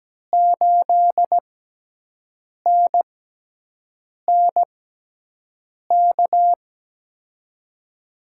Morse code, heard as a keyed tone, about 17 wpm, spaced as for 6 wpm, 700 Hz. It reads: 8NNK
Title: USA Call Signs - 5 Characters - 6x Character Spacing - 17wpm